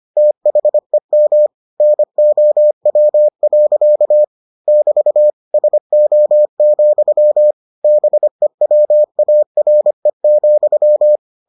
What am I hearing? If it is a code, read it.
Morse code, heard as a keyed tone, about 25 wpm, 600 Hz. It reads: THEM NOW. <BT> SO, BEWARE,